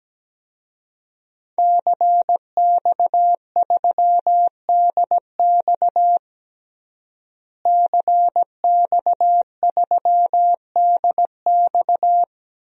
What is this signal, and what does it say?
Morse code, heard as a keyed tone, about 17 wpm, 700 Hz. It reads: CX3DX CX3DX
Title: DXCC-100 Call Signs - 5 Characters - Repeated 3x - 17wpm